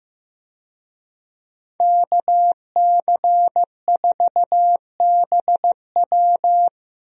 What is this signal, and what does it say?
Morse code, heard as a keyed tone, about 15 wpm, 700 Hz. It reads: KC4BW
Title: USA Call Signs - 5 Characters - Repeated 3x - 15wpm